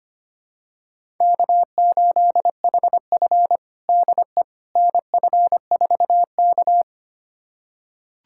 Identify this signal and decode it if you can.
Morse code, heard as a keyed tone, about 25 wpm, 700 Hz. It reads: K8HF DE NF4K
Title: USA Callsign DE Callsign - 4 Characters - 25wpm